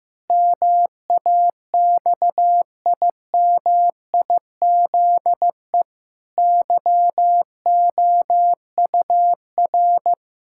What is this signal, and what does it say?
Morse code, heard as a keyed tone, about 15 wpm, 700 Hz. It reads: MAXIMIZE YOUR
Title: Pangram Challenge v2 - 15wpm